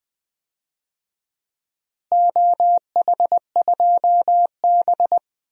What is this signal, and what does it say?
Morse code, heard as a keyed tone, about 20 wpm, 700 Hz. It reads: OH2B
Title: DXCC-100 Call Signs - 4 Characters - Repeated 3x - 20wpm